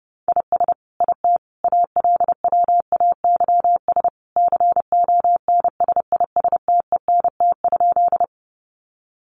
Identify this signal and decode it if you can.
Morse code, heard as a keyed tone, about 30 wpm, 700 Hz. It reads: IS IT ALWAYS CONSISTENT?
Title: Sentences from Top 1500 Words 30wpm